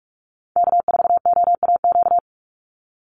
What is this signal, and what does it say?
Morse code, heard as a keyed tone, about 44 wpm, 700 Hz. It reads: K4OAQ
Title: Call Signs Repeated 3x - 45wpm